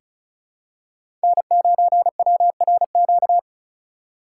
Morse code, heard as a keyed tone, about 36 wpm, 700 Hz. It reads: N9WRQ
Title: USA Call Signs - 5 Characters - 4x Character Spacing - 35wpm